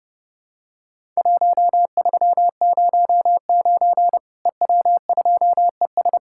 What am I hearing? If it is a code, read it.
Morse code, heard as a keyed tone, about 30 wpm, 700 Hz. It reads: 1309 EW2ES
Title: Worked All Europe DX Contest - QTC Exchange - 30wpm